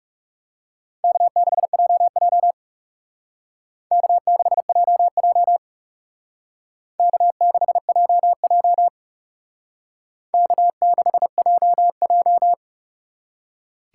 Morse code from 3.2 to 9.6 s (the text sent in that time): K6JJ K6JJ